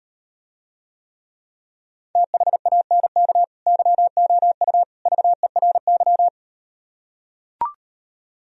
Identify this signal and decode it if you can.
Morse code, heard as a keyed tone, about 38 wpm, 700 Hz. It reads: THANK YOU VERY